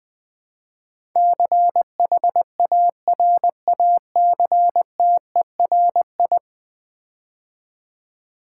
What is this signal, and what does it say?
Morse code, heard as a keyed tone, about 20 wpm, 700 Hz. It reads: CHARACTERI
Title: Build A Word - 5 Letters At A Time - 12-Letter Words - 20wpm